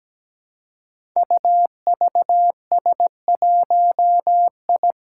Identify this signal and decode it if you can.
Morse code, heard as a keyed tone, about 17 wpm, 700 Hz. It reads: UVS1I
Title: Five Challenging Letter-Number Combinations 17wpm